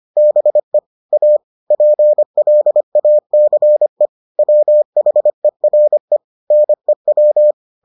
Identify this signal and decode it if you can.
Morse code, heard as a keyed tone, about 25 wpm, 600 Hz. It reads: BE A PLACE WHERE NEW